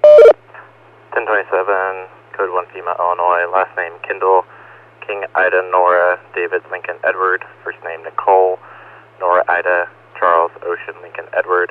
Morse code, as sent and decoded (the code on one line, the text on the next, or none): none